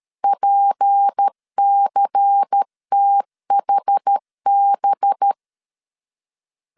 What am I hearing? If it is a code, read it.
Morse code, heard as a keyed tone, about 13 wpm, 800 Hz. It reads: PCTHB
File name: CW / DET / 1 6 1500C S B 12wpm 3431